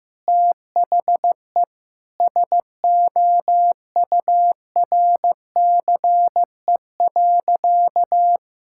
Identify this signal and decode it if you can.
Morse code, heard as a keyed tone, about 14 wpm, 700 Hz. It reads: THE SOURCE.
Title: Sentences from Top 1000 Words - Review Part 5 - 15wpm